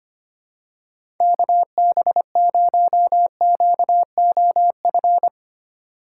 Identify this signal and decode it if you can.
Morse code, heard as a keyed tone, about 25 wpm, 700 Hz. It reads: KB0QOF
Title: USA Call Signs - 6 Characters - 6x Character Spacing - 25wpm